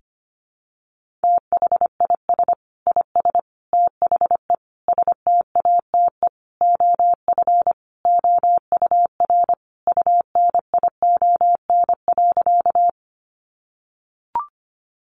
Morse code, heard as a keyed tone, about 25 wpm, 700 Hz. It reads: THIS IS THE STATE OF OUR UNION.
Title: Sentences from Top 1400 Words 25wpm